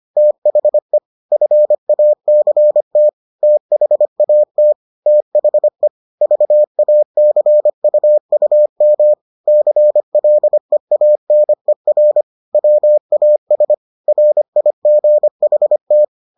Morse code, heard as a keyed tone, about 25 wpm, 600 Hz. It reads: THE FACT THAT THE VACUUM CLEANER WAS RIGHT